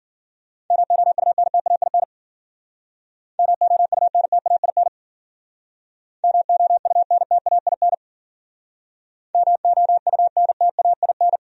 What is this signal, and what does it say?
Morse code, heard as a keyed tone, about 54 wpm, 700 Hz. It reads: MOUNTAIN MOUNTAIN MOUNTAIN MOUNTAIN